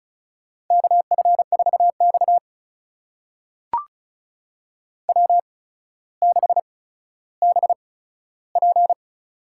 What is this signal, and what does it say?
Morse code, heard as a keyed tone, about 36 wpm, spaced as for 15 wpm, 700 Hz. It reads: KF4X W6BP
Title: USA Call Signs - 4 Characters - 8x Character Spacing - 35wpm